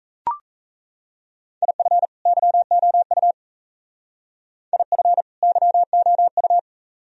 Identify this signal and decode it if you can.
Morse code, heard as a keyed tone, about 36 wpm, 700 Hz. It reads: IF YOU IF YOU